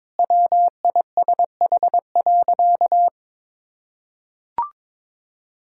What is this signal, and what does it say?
Morse code, heard as a keyed tone, about 22 wpm, 700 Hz. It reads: WISH.
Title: Sentences from Top 1600 Words 22wpm